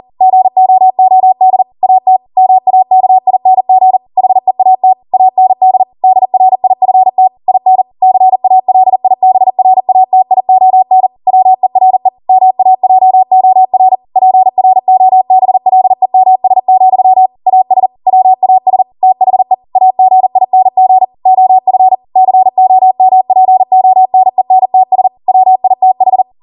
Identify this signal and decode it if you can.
Morse code, heard as a keyed tone, about 40 wpm, 750 Hz. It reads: GOOD AT MAKING HEAT AND DRIFT IN CALIBRATION WERE MAJOR PROBLEMS, AS WAS THE AGING OF COMPONENTS WITH